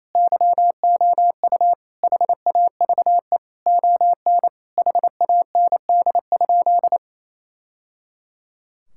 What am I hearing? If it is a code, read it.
Morse code, heard as a keyed tone, about 28 wpm, 700 Hz. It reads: YOU HAVE ON HAND?
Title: Sentences from Top 1000 Words - Review Part 5 - 28wpm